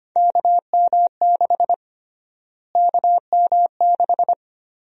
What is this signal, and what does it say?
Morse code, heard as a keyed tone, about 25 wpm, 700 Hz. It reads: KM6 KM6
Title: USA Call Sign Prefixes - Repeated 2x - 25wpm